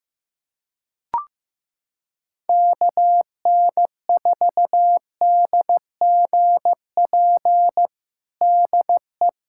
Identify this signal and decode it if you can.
Morse code, heard as a keyed tone, about 15 wpm, 700 Hz. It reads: KN4DGP DE
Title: USA Callsign DE Callsign - 6 Characters - 15wpm